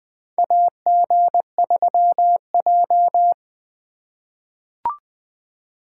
Morse code, heard as a keyed tone, about 20 wpm, 700 Hz. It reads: AG3J
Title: USA Call Signs - 4 Characters - 4x Character Spacing - 20wpm